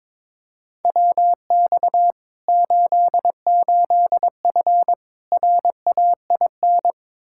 Morse code, heard as a keyed tone, about 21 wpm, 700 Hz. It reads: WX 88F RAIN
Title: Practice QSO - Exchange 2 - Info - 22wpm